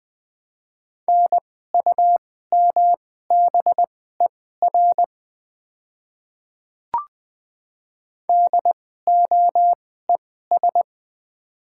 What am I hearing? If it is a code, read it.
Morse code, heard as a keyed tone, about 20 wpm, spaced as for 15 wpm, 690 Hz. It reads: NUMBER DOES
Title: 2x Character Spacing - Top 200 Words - 20wpm